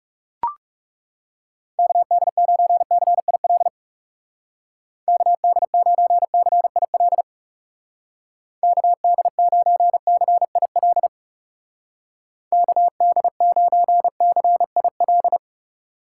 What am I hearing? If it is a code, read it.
Morse code, heard as a keyed tone, about 37 wpm, 700 Hz. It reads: KD9CIL KD9CIL KD9CIL KD9CIL
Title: Speed Racing - USA 2x3 Call Signs 30wpm